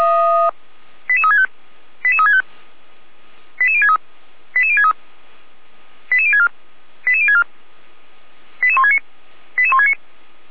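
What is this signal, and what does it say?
Five-tone selective call (ZVEI), heard as quick runs of five tones, tones of about 70 ms. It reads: 88365 88365 88063 88063 88064 88064 88169 88169
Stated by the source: ZVEI five-tone selective call received with an SDR.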